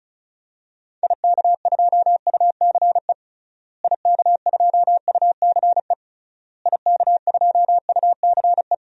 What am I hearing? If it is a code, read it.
Morse code, heard as a keyed tone, about 36 wpm, 700 Hz. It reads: IK2UCE IK2UCE IK2UCE
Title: DXCC-100 Call Signs - 6 Characters - Repeated 3x - 35wpm